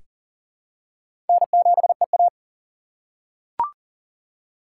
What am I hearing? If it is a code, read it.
Morse code, heard as a keyed tone, about 40 wpm, 700 Hz. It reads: N7EA
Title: USA Call Signs - 3x Character Spacing - 40wpm